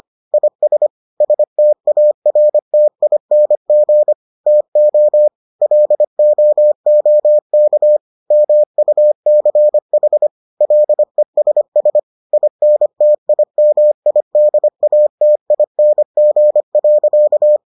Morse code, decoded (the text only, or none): IS STARTING TO LOOK MUCH LESS INTIMIDATING.